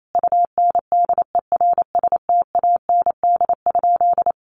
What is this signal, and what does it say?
Morse code, heard as a keyed tone, about 28 wpm, 700 Hz. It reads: UNDERSTAND?
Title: Sentences from Top 1500 Words 28wpm